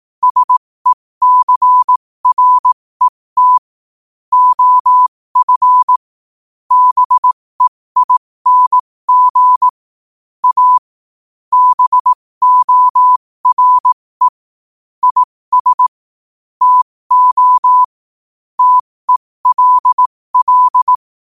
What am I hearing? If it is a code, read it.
Morse code, heard as a keyed tone, about 18 wpm, spaced as for 15 wpm, 1000 Hz. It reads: SECRET OF BEING A BORE IS TO TELL
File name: qotdfiles / 15wpm